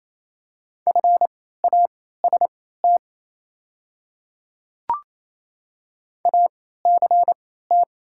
Morse code, heard as a keyed tone, about 28 wpm, spaced as for 16 wpm, 700 Hz. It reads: FAST ACT
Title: 3x Character Spacing - Top 500 Words - 28wpm